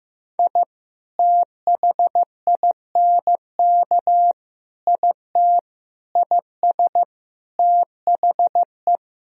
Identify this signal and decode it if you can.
Morse code, heard as a keyed tone, about 16 wpm, 700 Hz. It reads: I THINK IT IS THE
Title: Sentences from Top 1000 Words - Review Part 2 - 15wpm